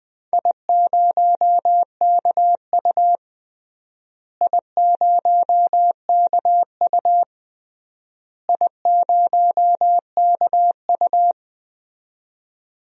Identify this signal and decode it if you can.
Morse code, heard as a keyed tone, about 21 wpm, 700 Hz. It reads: I0KU I0KU I0KU
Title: DXCC-100 Call Signs - 4 Characters - Repeated 3x - 20wpm